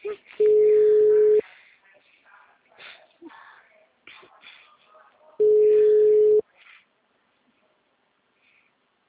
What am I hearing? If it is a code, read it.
Ringback tone (Europe, 425 Hz).